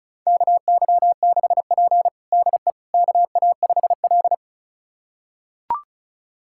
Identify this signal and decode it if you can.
Morse code, heard as a keyed tone, about 36 wpm, 700 Hz. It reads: KY6P DE KA5L
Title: USA Callsign DE Callsign - 4 Characters - 35wpm